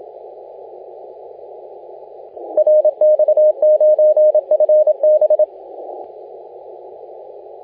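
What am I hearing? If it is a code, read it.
Morse code, heard as a keyed tone, about 27 wpm, 610 Hz, with radio noise. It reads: RX9FB
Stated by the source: Morse contest audio recorded by LA4C during the SAC CW 2018 contest.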